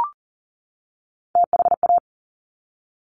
Morse code, heard as a keyed tone, about 38 wpm, 700 Hz. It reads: THA
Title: Top Three Letter Combinations in Words 40wpm